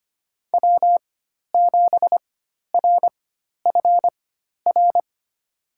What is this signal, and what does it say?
Morse code, heard as a keyed tone, about 25 wpm, spaced as for 12 wpm, 700 Hz. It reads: W7RFR
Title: USA Call Signs - 5 Characters - 4x Character Spacing - 25wpm